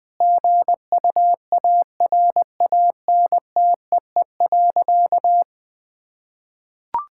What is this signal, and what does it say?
Morse code, heard as a keyed tone, about 20 wpm, 700 Hz. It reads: GUARANTEE.
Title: Sentences from Top 1600 Words 20wpm